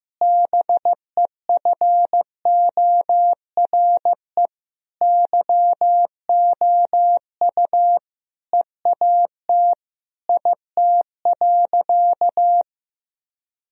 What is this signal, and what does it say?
Morse code, heard as a keyed tone, about 16 wpm, 700 Hz. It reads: BEFORE YOU EAT IT.
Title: Sentences from Top 1000 Words - Review Part 3 - 15wpm